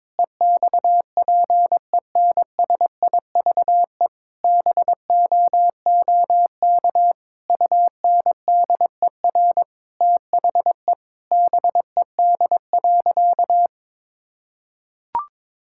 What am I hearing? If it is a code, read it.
Morse code, heard as a keyed tone, about 21 wpm, 700 Hz. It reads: EXPENSIVE BOOK UNDER THE BED.
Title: Sentences from Top 1000 Words - Review Part 6 - 22wpm